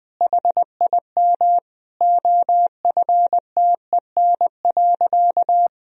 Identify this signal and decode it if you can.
Morse code, heard as a keyed tone, about 20 wpm, 700 Hz. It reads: HIM OFTEN.
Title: Sentences from Top 1000 Words - Review Part 1 - 20wpm